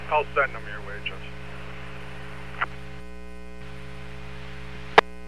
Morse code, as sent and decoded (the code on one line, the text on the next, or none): none